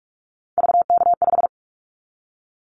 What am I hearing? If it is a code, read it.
Morse code, heard as a keyed tone, about 46 wpm, 700 Hz. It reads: VK5
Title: Callsign Prefixes for Top 50 DXCC - 45wpm